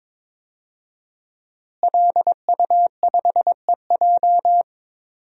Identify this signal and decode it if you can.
Morse code, heard as a keyed tone, about 22 wpm, 700 Hz. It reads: LU5EJ
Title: DXCC-100 Call Signs - 5 Characters - Repeated 2x - 22wpm